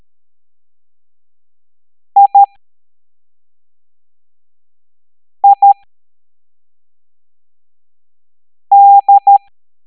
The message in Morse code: I I D